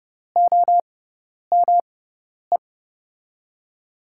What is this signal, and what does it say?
Morse code, heard as a keyed tone, about 30 wpm, spaced as for 10 wpm, 700 Hz. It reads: OME
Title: USA Call Signs - 6 Characters - 6x Character Spacing - 30wpm